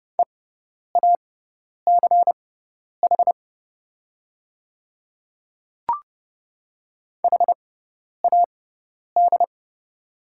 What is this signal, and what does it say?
Morse code, heard as a keyed tone, about 30 wpm, spaced as for 10 wpm, 700 Hz. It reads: EACH HAD
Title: 6x Character Spacing - Top 100 Words - 30wpm